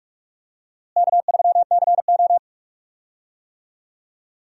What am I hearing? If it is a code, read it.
Morse code, heard as a keyed tone, about 43 wpm, 700 Hz. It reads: K3CO
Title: USA Call Signs - 4 Characters - Repeated 2x - 45wpm